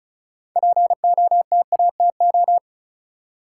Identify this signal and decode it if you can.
Morse code, heard as a keyed tone, about 36 wpm, 700 Hz. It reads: POTATO
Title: Six Letter Words - 35wpm